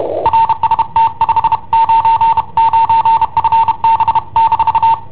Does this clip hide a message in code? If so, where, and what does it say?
Yes: Morse code from 0.3 to 4.9 s, heard as a keyed tone, about 31 wpm, 910 Hz. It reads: RST 599 FB-